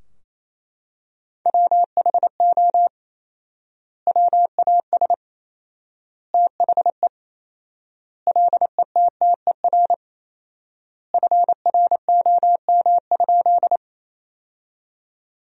Morse code, heard as a keyed tone, about 28 wpm, 700 Hz. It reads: WHO WAS THE LETTER FROM?
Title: 4x Word-Spacing - Sentences from Top 300 Words - 28wpm